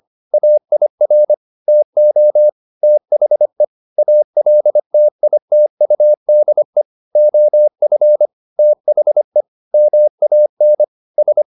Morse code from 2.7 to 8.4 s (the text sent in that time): THE ALTITUDE OF